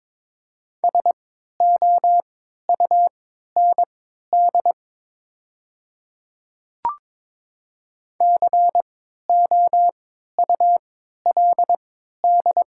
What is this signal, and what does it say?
Morse code, heard as a keyed tone, about 23 wpm, spaced as for 13 wpm, 700 Hz. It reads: SOUND COULD
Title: 3x Character Spacing - Top 100 Words - 22wpm